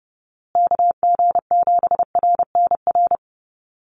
Morse code, heard as a keyed tone, about 30 wpm, 700 Hz. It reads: KG7RNR
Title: USA Call Signs Repeated 2x 30wpm